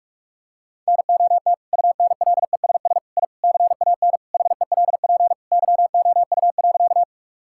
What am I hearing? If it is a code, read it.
Morse code, heard as a keyed tone, about 41 wpm, 700 Hz. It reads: NOT UNLESS I CAN HELP YOU.
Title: Sentences from Top 1000 Words - Review Part 6 - 45wpm